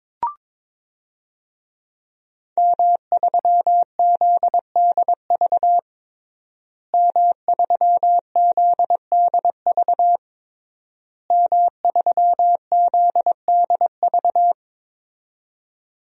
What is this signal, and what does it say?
Morse code, heard as a keyed tone, about 22 wpm, 700 Hz. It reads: M3ZDV M3ZDV M3ZDV
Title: DXCC-100 Call Signs - 5 Characters - Repeated 3x - 22wpm